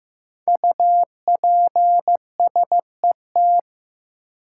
Morse code, heard as a keyed tone, about 15 wpm, 700 Hz. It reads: UPSET
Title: Five Letter Words - 15wpm